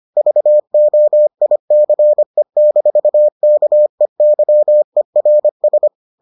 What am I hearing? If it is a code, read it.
Morse code, heard as a keyed tone, about 25 wpm, 600 Hz. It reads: VOICE-KEYERS